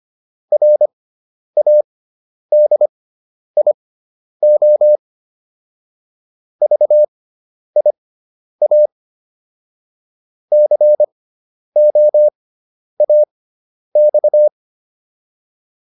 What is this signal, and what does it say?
Morse code, heard as a keyed tone, about 25 wpm, spaced as for 10 wpm, 600 Hz. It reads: RADIO VIA COAX